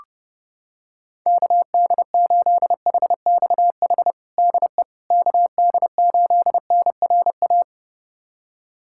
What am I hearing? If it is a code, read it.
Morse code, heard as a keyed tone, about 32 wpm, 700 Hz. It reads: KD8HXH DE KD8NRA
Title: USA Callsign DE Callsign 30wpm